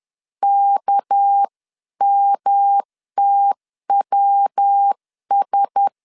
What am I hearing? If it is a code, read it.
Morse code, heard as a keyed tone, about 10 wpm, 800 Hz. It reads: KMTWS
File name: CW / DET / 1 6 1500C S C 10wpm 4252